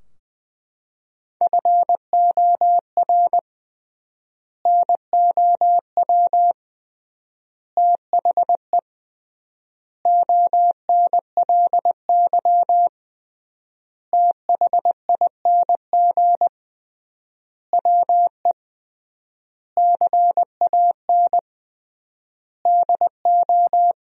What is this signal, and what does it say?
Morse code, heard as a keyed tone, about 20 wpm, 700 Hz. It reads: FOR NOW THE ONLY THING WE CAN DO
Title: 3x Word-Spacing - Sentences from Top 200 Words - 20wpm